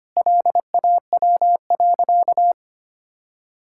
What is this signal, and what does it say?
Morse code, heard as a keyed tone, about 25 wpm, 700 Hz. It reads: LAW.
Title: Sentences from Top 1600 Words 25wpm